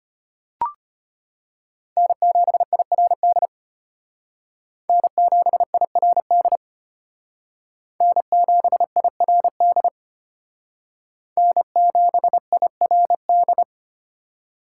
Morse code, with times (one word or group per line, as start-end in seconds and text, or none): 2.0-3.5 N7IRD
4.9-6.6 N7IRD
8.0-9.9 N7IRD
11.4-13.6 N7IRD